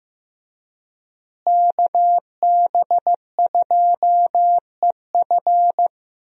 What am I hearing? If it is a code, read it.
Morse code, heard as a keyed tone, about 15 wpm, 700 Hz. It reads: KB2EF
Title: USA Call Signs - 5 Characters - Repeated 2x - 15wpm